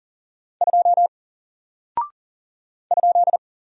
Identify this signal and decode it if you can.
Morse code, heard as a keyed tone, about 40 wpm, 700 Hz. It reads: 2 ?
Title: Single Character (Letter:Number:Punctuation:Prosign) 40wpm